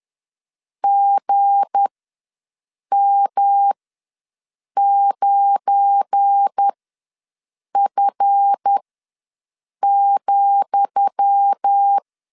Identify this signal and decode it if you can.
Morse code, heard as a keyed tone, about 10 wpm, spaced as for 6 wpm, 800 Hz. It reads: GM9F,
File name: LEZIONI / 10 SIGNS / 10D COMA ws 10wpm 4314